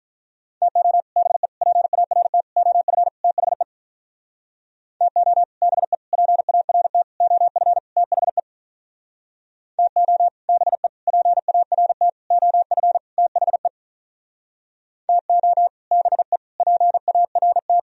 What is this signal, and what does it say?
Morse code, heard as a keyed tone, about 43 wpm, 700 Hz. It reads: TO BE PART OF THE TO BE PART OF THE TO BE PART OF THE TO BE PART
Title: Speed-Racing - Sets of 5 Words 35wpm